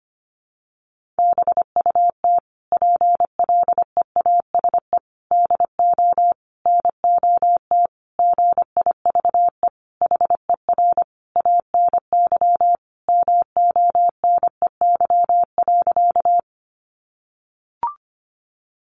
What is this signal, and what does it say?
Morse code, heard as a keyed tone, about 25 wpm, 700 Hz. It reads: BUT PLEASE DO NOT GIVE HER ANY MONEY.
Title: Sentences from Top 1500 Words 25wpm